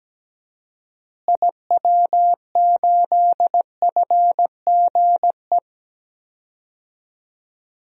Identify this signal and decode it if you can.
Morse code, heard as a keyed tone, about 17 wpm, 700 Hz. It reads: IW8FGE
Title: DXCC-100 Call Signs - 6 Characters - Repeated 2x - 17wpm